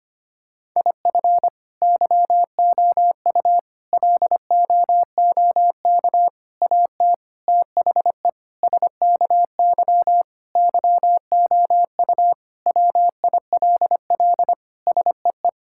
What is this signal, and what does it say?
Morse code, heard as a keyed tone, about 25 wpm, 700 Hz. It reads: IF YOU LOOK AT THE SKY YOU WILL SEE
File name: practice / 25wpm / Sentences with 4-Word Phrases - 25wpm